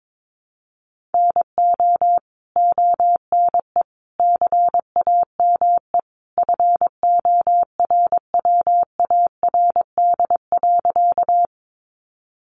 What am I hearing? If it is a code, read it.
Morse code, heard as a keyed tone, about 22 wpm, 700 Hz. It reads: NO ONE CAME FORWARD.
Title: Sentences from Top 1300 Words 22wpm